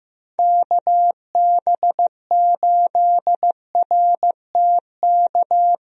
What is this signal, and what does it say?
Morse code, heard as a keyed tone, about 16 wpm, 700 Hz. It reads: KB8RTK
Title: USA Call Signs - 6 Characters - Repeated 2x - 15wpm